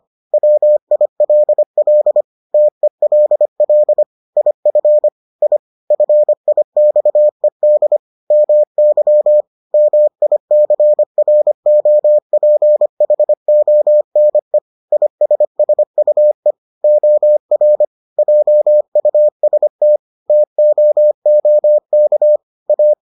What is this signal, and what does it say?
Morse code, heard as a keyed tone, about 25 wpm, 600 Hz. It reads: WILL TELL IF I FIXED MY MICROPHONE ISSUE OR JUST TOOK A